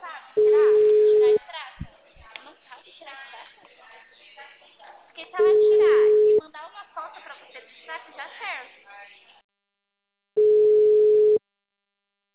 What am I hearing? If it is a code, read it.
Ringback tone (Europe, 425 Hz), with speech.
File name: out- -2013- - - .92446